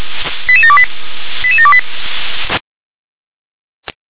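Five-tone selective call (ZVEI), heard as quick runs of five tones, tones of about 70 ms. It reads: 88728 88728
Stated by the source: ZVEI five-tone call 88730 received on SDR.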